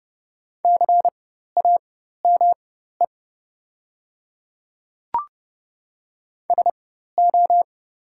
Morse code, heard as a keyed tone, about 30 wpm, spaced as for 14 wpm, 700 Hz. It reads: CAME SO
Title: 4x Character Spacing - Top 200 Words - 30wpm